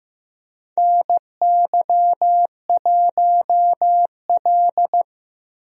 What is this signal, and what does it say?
Morse code, heard as a keyed tone, about 15 wpm, 700 Hz. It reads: NY1L
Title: USA Call Signs - 4 Characters - 15wpm